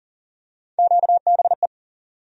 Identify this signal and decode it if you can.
Morse code, heard as a keyed tone, about 38 wpm, 700 Hz. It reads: QBE
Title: USA Call Sign Postfixes - 3 Letters 40wpm